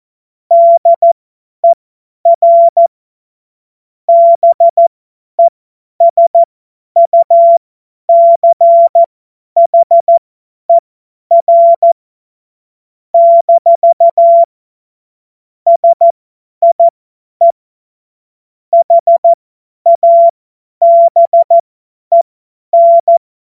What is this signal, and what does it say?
Morse code, heard as a keyed tone, about 14 wpm, spaced as for 10 wpm, 680 Hz. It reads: DER BESUCHER - SIE HABEN
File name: Witz-14-Oktober-2025-50-F70